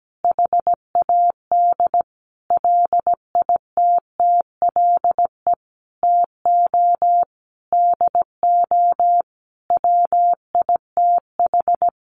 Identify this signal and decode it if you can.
Morse code, heard as a keyed tone, about 17 wpm, 700 Hz. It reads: HAD LITTLE TO DO WITH
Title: Sentences from Top 1500 Words 17wpm